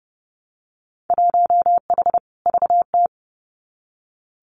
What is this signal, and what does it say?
Morse code, heard as a keyed tone, about 30 wpm, 700 Hz. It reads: 1H VT
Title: Winter Field Day Exchanges 30wpm